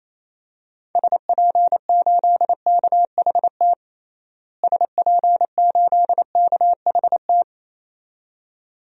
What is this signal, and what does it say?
Morse code, heard as a keyed tone, about 27 wpm, 700 Hz. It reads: SP8KHT SP8KHT
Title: DXCC-100 Call Signs - 6 Characters - Repeated 2x - 28wpm